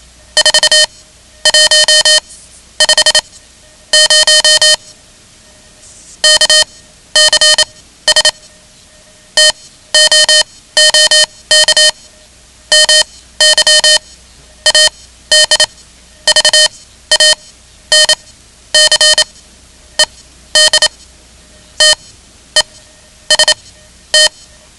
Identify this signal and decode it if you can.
Morse code, heard as a keyed tone, about 27 wpm, spaced as for 12 wpm, 3050 Hz. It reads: 4150 KCS TOOK MY ADVANCED TEST
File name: ham radio / CodeQuick / Code Quick 10